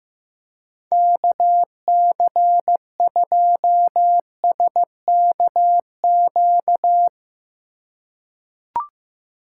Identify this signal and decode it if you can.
Morse code, heard as a keyed tone, about 15 wpm, 700 Hz. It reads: KC2SKQ
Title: USA Call Signs - 6 Characters - 8x Character Spacing - 15wpm